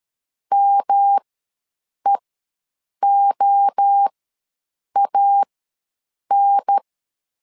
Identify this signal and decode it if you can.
Morse code, heard as a keyed tone, about 13 wpm, spaced as for 7 wpm, 800 Hz. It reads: MEOAN
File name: LEZIONI / 3 MWRO / 3 MWRO ws 12wpm 1914